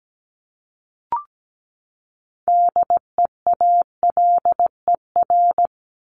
Morse code, heard as a keyed tone, about 17 wpm, 700 Hz. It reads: DEALER
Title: Top 1001-1250 Words 17wpm